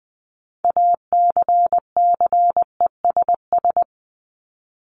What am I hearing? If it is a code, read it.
Morse code, heard as a keyed tone, about 20 wpm, 700 Hz. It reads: ACCESS
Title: Top 501-750 Words 20wpm